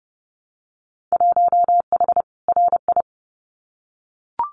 Morse code, heard as a keyed tone, about 30 wpm, 700 Hz. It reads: 1H RI
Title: Winter Field Day Exchanges 30wpm